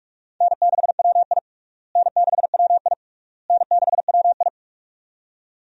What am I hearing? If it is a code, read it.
Morse code, heard as a keyed tone, about 41 wpm, 700 Hz. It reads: N6WI N6WI N6WI
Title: USA Call Signs - 4 Characters - Repeated 3x - 45wpm